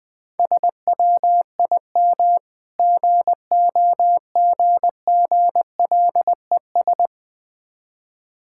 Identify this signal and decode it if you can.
Morse code, heard as a keyed tone, about 20 wpm, 700 Hz. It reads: SWIM GOGGLES
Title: Difficult Words with Hints - 20wpm